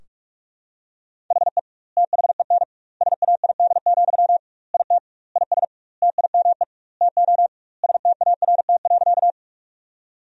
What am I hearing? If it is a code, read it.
Morse code, heard as a keyed tone, about 47 wpm, 700 Hz. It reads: HE THEN SAID, IT IS TIME TO START.
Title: 2x Word-Spacing - Sentences from Top 300 Words - 45wpm